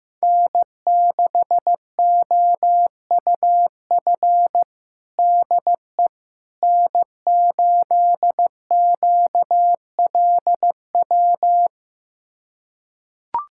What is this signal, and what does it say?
Morse code, heard as a keyed tone, about 16 wpm, 700 Hz. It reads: N6OUF DE N8QLW
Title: USA Callsign DE Callsign - 5 Characters - 15wpm